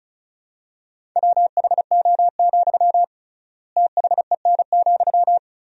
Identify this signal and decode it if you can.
Morse code, heard as a keyed tone, about 34 wpm, 700 Hz. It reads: WHO, THEN,
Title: 3x Word-Spacing - Sentences from Top 100 Words - 35wpm